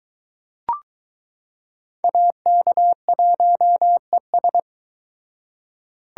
Morse code, heard as a keyed tone, about 20 wpm, 700 Hz. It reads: AK1ES